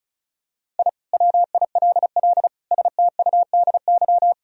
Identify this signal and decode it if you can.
Morse code, heard as a keyed tone, about 34 wpm, 700 Hz. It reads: I WILL STUDY
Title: Sentences from Top 1000 Words - Review Part 2 - 35wpm